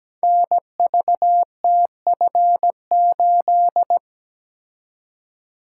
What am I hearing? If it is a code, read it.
Morse code, heard as a keyed tone, about 17 wpm, 700 Hz. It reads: NVTF8
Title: Code Groups - 5 Letter:Numbers - 17wpm